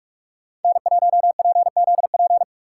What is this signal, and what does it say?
Morse code, heard as a keyed tone, about 45 wpm, 700 Hz. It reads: N1PZP
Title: USA Call Signs - 5 Characters - 3x Character Spacing - 45wpm